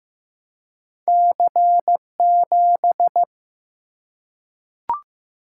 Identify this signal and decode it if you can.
Morse code, heard as a keyed tone, about 15 wpm, 700 Hz. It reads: C7
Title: Single-Letter Single-Number 15wpm